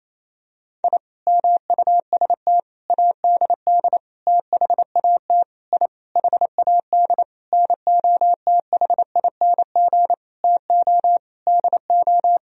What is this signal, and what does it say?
Morse code, heard as a keyed tone, about 28 wpm, 700 Hz. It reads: I MUST ADD THAT I HAD NOTHING TO DO
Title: Sentences from Top 1000 Words - Review Part 1 - 28wpm